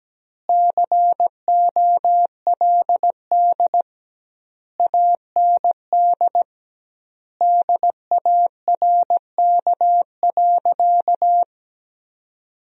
Morse code, heard as a keyed tone, about 17 wpm, 700 Hz. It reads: COLD AND DARK.
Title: 2x Word-Spacing - Sentences from Top 500 Words - 17wpm